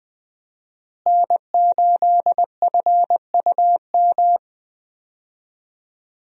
Morse code, heard as a keyed tone, about 20 wpm, 700 Hz. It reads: N8FUM
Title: USA Call Signs - 5 Characters - Repeated 2x - 20wpm